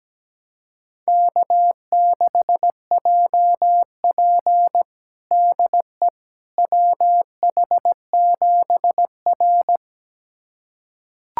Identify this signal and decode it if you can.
Morse code, heard as a keyed tone, about 17 wpm, 700 Hz. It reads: K6JP DE WH7R
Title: USA Callsign DE Callsign - 4 Characters - 17wpm